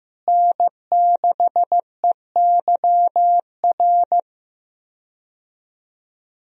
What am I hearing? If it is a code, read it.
Morse code, heard as a keyed tone, about 16 wpm, 700 Hz. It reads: N6EYR
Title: USA Call Signs - 5 Characters - Repeated 2x - 15wpm